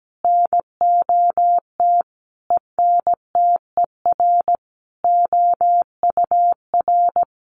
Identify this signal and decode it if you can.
Morse code, heard as a keyed tone, about 17 wpm, 700 Hz. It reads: NOT ENTER OUR
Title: Sentences from Top 1300 Words 17wpm